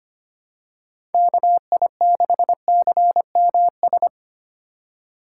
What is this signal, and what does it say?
Morse code, heard as a keyed tone, about 25 wpm, 700 Hz. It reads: KI6CMS
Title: USA Call Signs - 3x Character Spacing - 25wpm